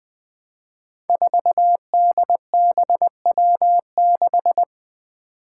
Morse code, heard as a keyed tone, about 20 wpm, 700 Hz. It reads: 4DBW6
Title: Five Challenging Letter-Number Combinations 20wpm